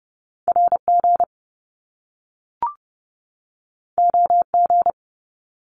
Morse code, heard as a keyed tone, about 30 wpm, 700 Hz. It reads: RG OG
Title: Top Two Letter Combinations in Words 30wpm